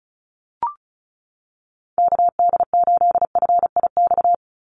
Morse code, heard as a keyed tone, about 36 wpm, 700 Hz. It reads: KD8FIX
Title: USA Call Signs 35wpm